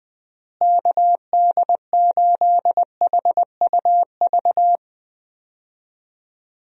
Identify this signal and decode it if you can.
Morse code, heard as a keyed tone, about 21 wpm, 700 Hz. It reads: KD8HUV
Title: USA Call Signs - 6 Characters - Repeated 2x - 20wpm